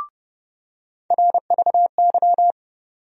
Morse code, heard as a keyed tone, about 30 wpm, 700 Hz. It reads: RVY
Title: USA Call Sign Postfixes - 3 Letters 30wpm